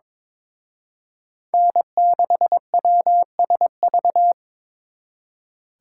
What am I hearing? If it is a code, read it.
Morse code, heard as a keyed tone, about 26 wpm, 700 Hz. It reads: N6WSV
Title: Speed Racing - USA 1x3 Call Signs 22wpm